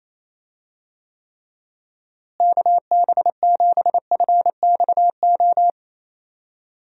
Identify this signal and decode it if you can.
Morse code, heard as a keyed tone, about 28 wpm, 700 Hz. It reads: KB7FXO
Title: USA Call Signs - 6 Characters - Repeated 2x - 28wpm